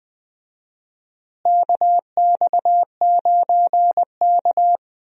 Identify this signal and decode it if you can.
Morse code, heard as a keyed tone, about 20 wpm, 700 Hz. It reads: KX9K
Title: USA Call Signs - 4 Characters - Repeated 2x - 20wpm